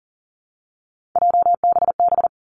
Morse code, heard as a keyed tone, about 41 wpm, 700 Hz. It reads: JBB
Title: Three Challenging Letter-Number Combinations 40wpm